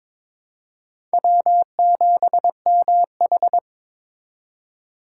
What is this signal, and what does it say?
Morse code, heard as a keyed tone, about 22 wpm, 700 Hz. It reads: W7MH
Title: USA Call Signs - 4 Characters - 4x Character Spacing - 22wpm